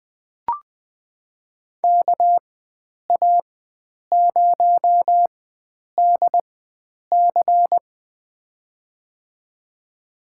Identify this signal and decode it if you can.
Morse code, heard as a keyed tone, about 20 wpm, spaced as for 9 wpm, 700 Hz. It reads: KA0DC
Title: USA Call Signs - 5 Characters - 4x Character Spacing - 20wpm